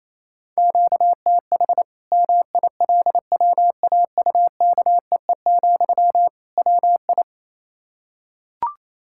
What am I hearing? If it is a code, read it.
Morse code, heard as a keyed tone, about 28 wpm, 700 Hz. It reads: QTH MILWAUKEE, WI
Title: Practice QSO - Exchange 1 - Beginning - 28wpm